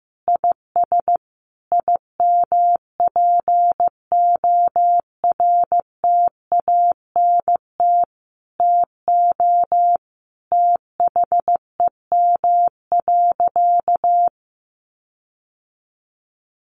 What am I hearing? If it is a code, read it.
Morse code, heard as a keyed tone, about 14 wpm, 700 Hz. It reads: IS IMPORTANT TO THEM.
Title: Sentences from Top 1400 Words 15wpm